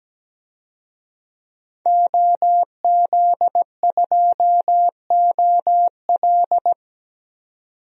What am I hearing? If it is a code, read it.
Morse code, heard as a keyed tone, about 17 wpm, 700 Hz. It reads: OZ2OL
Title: DXCC-100 Call Signs - 5 Characters - 17wpm